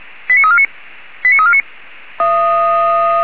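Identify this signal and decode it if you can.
Five-tone selective call (ZVEI), heard as quick runs of five tones, tones of about 70 ms. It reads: 89349 89349
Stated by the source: ZVEI five-tone selective call received on an SDR.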